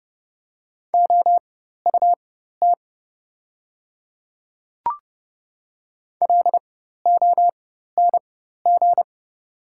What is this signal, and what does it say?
Morse code, heard as a keyed tone, about 30 wpm, spaced as for 14 wpm, 700 Hz. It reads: OUT LONG
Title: 4x Character Spacing - Top 100 Words - 30wpm